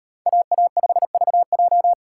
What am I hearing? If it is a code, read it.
Morse code, heard as a keyed tone, about 37 wpm, 700 Hz. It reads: AA5VJ